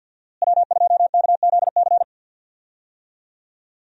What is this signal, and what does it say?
Morse code, heard as a keyed tone, about 51 wpm, 690 Hz. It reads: W2KZC
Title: USA Call Signs - 5 Characters - Repeated 3x - 50wpm